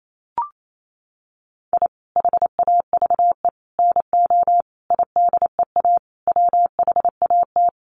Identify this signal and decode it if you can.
Morse code, heard as a keyed tone, about 28 wpm, 700 Hz. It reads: I HAVE NO IDEA WHAT
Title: Sets of 5 Words Encore 28wpm